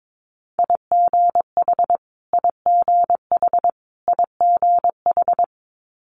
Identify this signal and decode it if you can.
Morse code, heard as a keyed tone, about 22 wpm, 700 Hz. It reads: IGH IGH IGH
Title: Multi-letter Phonemes Repeated 3x 22wpm